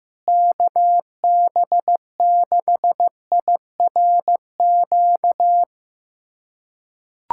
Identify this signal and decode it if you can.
Morse code, heard as a keyed tone, about 15 wpm, 700 Hz. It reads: KB6IRQ
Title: USA Callsign DE Callsign 15wpm